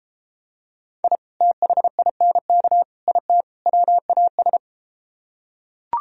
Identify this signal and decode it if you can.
Morse code, heard as a keyed tone, about 32 wpm, 700 Hz. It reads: I THINK IT WAS